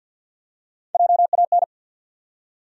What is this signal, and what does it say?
Morse code, heard as a keyed tone, about 53 wpm, 700 Hz. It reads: JAN
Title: USA Common Names - Max 3 Letters - 50wpm